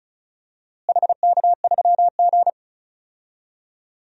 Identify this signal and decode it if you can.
Morse code, heard as a keyed tone, about 36 wpm, 700 Hz. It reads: HK3G
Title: DXCC-100 Call Signs - 4 Characters - Repeated 3x - 35wpm